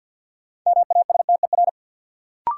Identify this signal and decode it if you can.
Morse code, heard as a keyed tone, about 45 wpm, 700 Hz. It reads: MASTER
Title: Sentences from Top 1300 Words - New Word Drill Alt - 50wpm